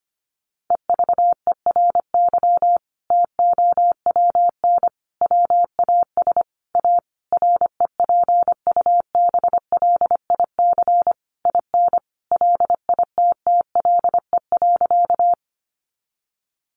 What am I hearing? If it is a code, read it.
Morse code, heard as a keyed tone, about 25 wpm, 700 Hz. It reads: EVERY TOWN WAS A REPUBLIC IN LITTLE.
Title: Sentences from Top 1500 Words 25wpm